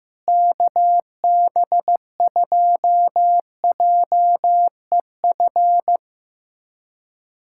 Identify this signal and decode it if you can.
Morse code, heard as a keyed tone, about 15 wpm, 700 Hz. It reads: KB2JEF
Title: USA Callsign DE Callsign - 6 Characters - 15wpm